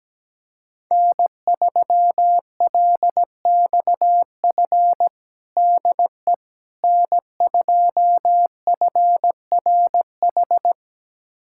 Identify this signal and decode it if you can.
Morse code, heard as a keyed tone, about 17 wpm, 700 Hz. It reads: N3LXF DE N2FRH
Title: USA Callsign DE Callsign - 5 Characters - 17wpm